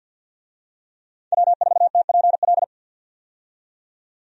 Morse code, heard as a keyed tone, about 50 wpm, 700 Hz. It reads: W4TPL
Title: USA Call Signs - 5 Characters - 50wpm